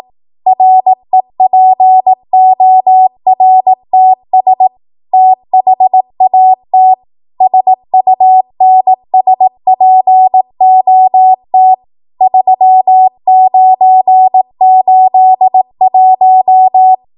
Morse code, heard as a keyed tone, about 18 wpm, 750 Hz. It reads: REPORTS THAT SUNSPOT 3981
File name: Morse / 18WPM